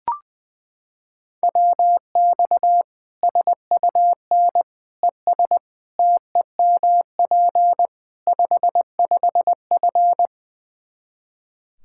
Morse code, heard as a keyed tone, about 19 wpm, 700 Hz. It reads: WX SUN ES TEMP 55F
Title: Practice QSO - Exchange 2 - Info - 20wpm